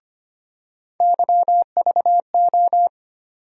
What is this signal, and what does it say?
Morse code, heard as a keyed tone, about 25 wpm, 700 Hz. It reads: YVO
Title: USA Call Sign Postfixes - 3 Letters 25wpm